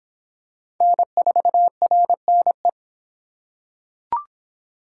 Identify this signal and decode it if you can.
Morse code, heard as a keyed tone, about 25 wpm, 700 Hz. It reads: N4RNE